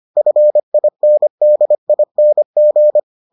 Morse code, heard as a keyed tone, about 25 wpm, 600 Hz. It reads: FINDING